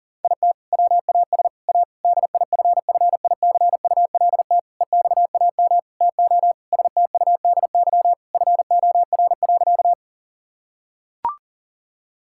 Morse code, heard as a keyed tone, about 40 wpm, 690 Hz. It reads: IT WAS A DIFFICULT EXAM TO STUDY FOR.
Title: Sentences from Top 1000 Words - Review Part 5 - 40wpm